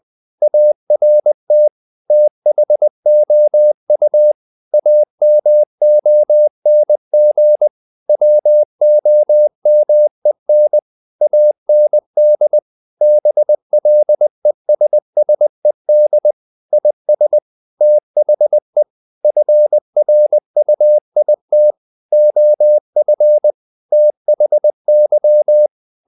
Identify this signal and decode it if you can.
Morse code, heard as a keyed tone, about 20 wpm, 600 Hz. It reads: ART THOU AMONG WOMEN AND BLESSED IS THE FRUIT OF THY
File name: the old man and the sea / the old man and the sea part 1 20wpm